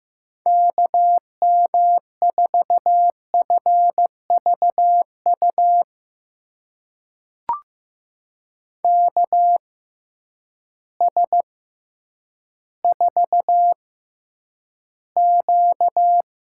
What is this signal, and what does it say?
Morse code, heard as a keyed tone, about 15 wpm, spaced as for 9 wpm, 700 Hz. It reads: KM4FVU KS4Q
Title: USA Call Signs - 6x Character Spacing - 15wpm